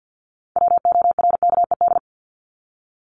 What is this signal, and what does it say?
Morse code, heard as a keyed tone, about 50 wpm, 700 Hz. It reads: WORKED
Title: ED Suffix - Top 100 Words with Base Words 50wpm